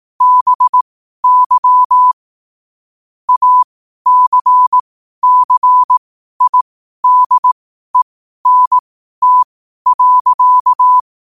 Morse code, heard as a keyed tone, about 18 wpm, spaced as for 12 wpm, 1000 Hz. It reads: BY ACCIDENT.
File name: qotdfiles / 12wpm